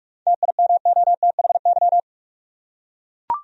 Morse code, heard as a keyed tone, about 42 wpm, 700 Hz. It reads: TIMOTHY
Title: USA Common Names - 45wpm